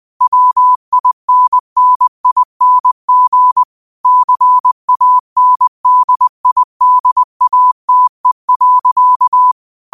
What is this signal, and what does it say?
Morse code, heard as a keyed tone, about 20 wpm, 1000 Hz. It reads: WINNING CANDIDATE.